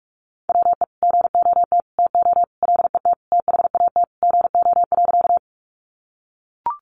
Morse code, heard as a keyed tone, about 44 wpm, 700 Hz. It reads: WE GOT TO LET THAT GO.
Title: Sentences from Top 300 Words 45wpm